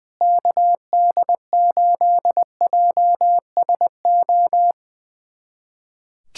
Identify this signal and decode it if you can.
Morse code, heard as a keyed tone, about 21 wpm, 700 Hz. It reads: KD8JSO
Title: USA Callsign DE Callsign 20wpm